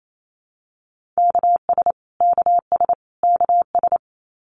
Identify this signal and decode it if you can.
Morse code, heard as a keyed tone, about 28 wpm, 700 Hz. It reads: KS KS KS
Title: US States Repeated 3x 28wpm